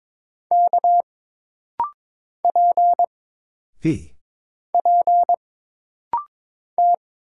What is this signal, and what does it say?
Morse code, heard as a keyed tone, about 22 wpm, 700 Hz. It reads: K P P T
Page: This rapid-fire timing is geared to push your ICR (Instant Character Recognition) skills.